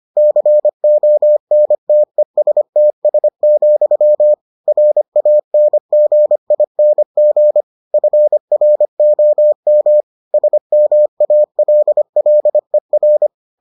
Morse code, heard as a keyed tone, about 25 wpm, 600 Hz. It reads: CONTESTS, RANGING FROM SMALLER